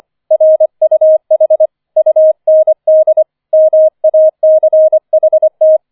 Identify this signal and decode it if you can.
Morse code, heard as a keyed tone, about 25 wpm, 630 Hz. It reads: RUH UND MACHT
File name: 005 Fünfter Streich 25wpm